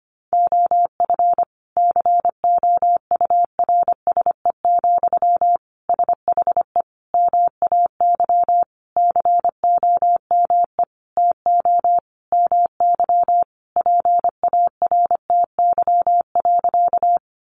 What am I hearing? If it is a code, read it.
Morse code, heard as a keyed tone, about 25 wpm, 700 Hz. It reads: OF COURSE, SHE MAY COME TO MY PARTY.
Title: Sentences from Top 1400 Words 25wpm